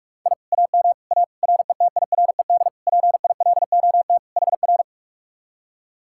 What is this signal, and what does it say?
Morse code, heard as a keyed tone, about 49 wpm, 700 Hz. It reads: I AM A RETIRED PILOT HR